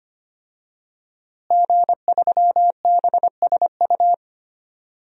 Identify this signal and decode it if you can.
Morse code, heard as a keyed tone, about 25 wpm, 700 Hz. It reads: G3BSU
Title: DXCC-100 Call Signs - 5 Characters - Repeated 3x - 25wpm